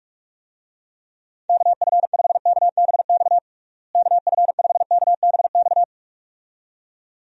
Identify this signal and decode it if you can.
Morse code, heard as a keyed tone, about 46 wpm, 700 Hz. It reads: KF5KBX KF5KBX
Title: USA Call Signs - 6 Characters - Repeated 2x - 45wpm